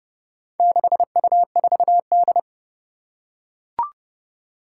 Morse code, heard as a keyed tone, about 30 wpm, 700 Hz. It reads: 6U4D
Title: Four Challenging Letter-Number Combinations 30wpm